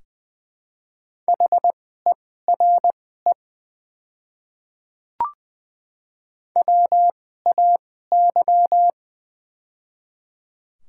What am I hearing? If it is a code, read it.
Morse code, heard as a keyed tone, about 20 wpm, spaced as for 15 wpm, 700 Hz. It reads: HERE WAY
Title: 2x Character Spacing - Top 200 Words - 20wpm